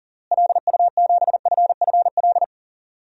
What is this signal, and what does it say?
Morse code, heard as a keyed tone, about 40 wpm, 700 Hz. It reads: LU7FFL
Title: DXCC-100 Call Signs - 6 Characters - Repeated 3x - 40wpm